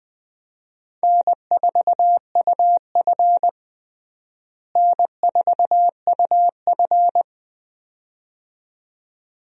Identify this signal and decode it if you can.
Morse code, heard as a keyed tone, about 20 wpm, 700 Hz. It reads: N4UF N4UF
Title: USA Call Signs - 4 Characters - Repeated 2x - 20wpm